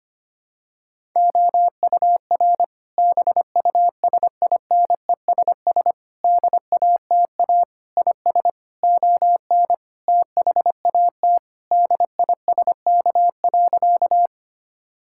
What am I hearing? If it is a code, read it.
Morse code, heard as a keyed tone, about 25 wpm, 700 Hz. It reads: OUR BUSINESS DATA IS ON THAT DISK.
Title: Sentences from Top 1000 Words - Review Part 6 - 25wpm